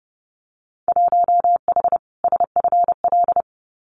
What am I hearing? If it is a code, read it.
Morse code, heard as a keyed tone, about 30 wpm, 700 Hz. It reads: 1H SFL
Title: Winter Field Day Exchanges 30wpm